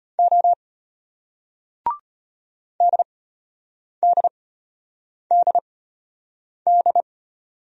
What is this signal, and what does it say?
Morse code, heard as a keyed tone, about 33 wpm, spaced as for 8 wpm, 700 Hz. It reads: O DDDD